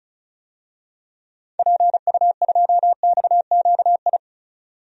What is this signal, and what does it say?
Morse code, heard as a keyed tone, about 34 wpm, 700 Hz. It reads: PU2XQI
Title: DXCC-100 Call Signs - 6 Characters - 35wpm